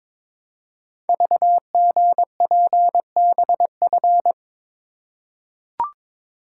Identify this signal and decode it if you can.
Morse code, heard as a keyed tone, about 22 wpm, 700 Hz. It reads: VGPBF
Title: Code Groups - 5 Letters - 22wpm